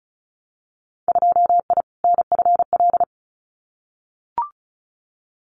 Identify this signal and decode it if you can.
Morse code, heard as a keyed tone, about 34 wpm, 700 Hz. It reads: 2I NFL
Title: Winter Field Day Exchanges 35wpm